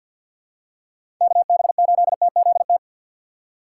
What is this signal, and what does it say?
Morse code, heard as a keyed tone, about 52 wpm, 700 Hz. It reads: KB8TZT
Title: USA Call Signs - 6 Characters - 3x Character Spacing - 50wpm